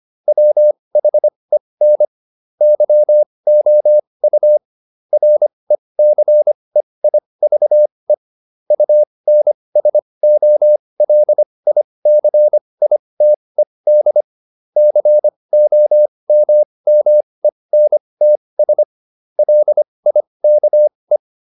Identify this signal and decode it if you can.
Morse code, heard as a keyed tone, about 25 wpm, spaced as for 20 wpm, 600 Hz. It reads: WHEN YOU RECEIVE UNSOLICITED COMMENTS LIKE